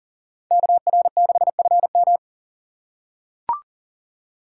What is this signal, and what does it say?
Morse code, heard as a keyed tone, about 39 wpm, 700 Hz. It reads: KR6FM
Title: USA Call Signs - 5 Characters - 2x Character Spacing - 40wpm